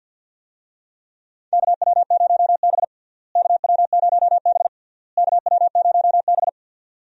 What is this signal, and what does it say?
Morse code, heard as a keyed tone, about 51 wpm, 700 Hz. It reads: KW0B KW0B KW0B
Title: USA Call Signs - 4 Characters - Repeated 3x - 50wpm